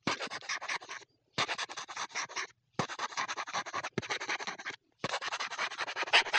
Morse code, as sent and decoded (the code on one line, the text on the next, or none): none